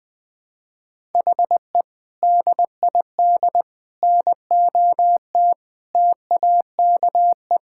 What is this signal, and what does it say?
Morse code, heard as a keyed tone, about 20 wpm, 700 Hz. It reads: HE DID NOT TAKE
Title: Sentences from Top 1000 Words - Review Part 6 - 20wpm